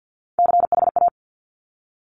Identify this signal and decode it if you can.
Morse code, heard as a keyed tone, about 50 wpm, 700 Hz. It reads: CHA
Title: Top Three Letter Combinations in Words 50wpm